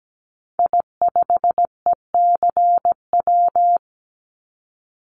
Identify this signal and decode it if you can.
Morse code, heard as a keyed tone, about 17 wpm, 700 Hz. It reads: I5ECW
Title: Call Signs - 17wpm